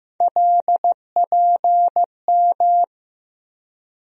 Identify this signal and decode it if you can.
Morse code, heard as a keyed tone, about 15 wpm, 700 Hz. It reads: LPM
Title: USA Call Sign Postfixes - 3 Letters 15wpm